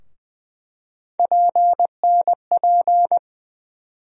Morse code, heard as a keyed tone, about 20 wpm, 700 Hz. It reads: PNP
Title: USA Call Sign Postfixes - 3 Letters 20wpm